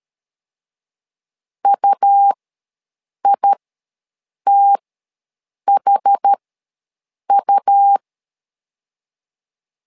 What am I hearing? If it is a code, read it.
Morse code, heard as a keyed tone, about 13 wpm, spaced as for 7 wpm, 800 Hz. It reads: UITHU
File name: LEZIONI / 2 UHDN / 2 UH ws 12wpm 1730